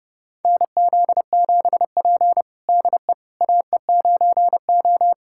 Morse code, heard as a keyed tone, about 30 wpm, 700 Hz. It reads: NZ7P DE AE9O
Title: USA Callsign DE Callsign - 4 Characters - 30wpm